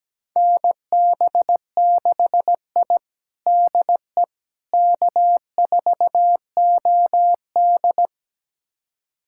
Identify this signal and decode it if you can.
Morse code, heard as a keyed tone, about 17 wpm, 700 Hz. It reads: NB6I DE K4OD
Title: USA Callsign DE Callsign - 4 Characters - 17wpm